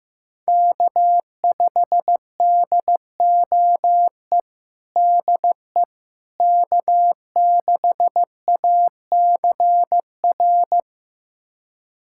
Morse code, heard as a keyed tone, about 14 wpm, 700 Hz. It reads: K5DOE DE K6ACR
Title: USA Callsign DE Callsign - 5 Characters - 15wpm